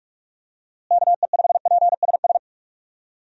Morse code, heard as a keyed tone, about 42 wpm, 700 Hz. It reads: KE5PSS
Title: USA Call Signs - 6 Characters - Repeated 3x - 45wpm